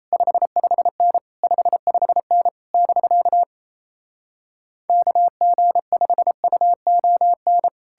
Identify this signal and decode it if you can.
Morse code, heard as a keyed tone, about 37 wpm, 700 Hz. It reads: 55N 55N <BK> KG5UON